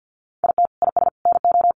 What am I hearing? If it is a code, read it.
Morse code, heard as a keyed tone, about 50 wpm, 700 Hz. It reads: IT IS NO
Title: Sentences from Top 1200 Words 50wpm